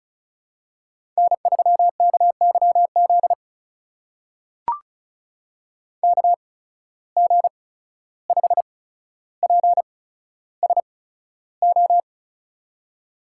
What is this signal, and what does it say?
Morse code, heard as a keyed tone, about 36 wpm, spaced as for 14 wpm, 700 Hz. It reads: N3KYZ KG5PSO
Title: USA Call Signs - 8x Character Spacing - 35wpm